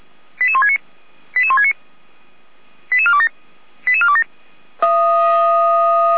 Five-tone selective call (ZVEI), heard as quick runs of five tones, tones of about 80 ms. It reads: 88169 88169 88427 88427
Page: SDR/ZVEI